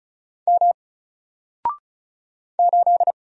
Morse code, heard as a keyed tone, about 35 wpm, 700 Hz. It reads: M 8
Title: Single Character (Letter:Number:Punctuation:Prosign) 35wpm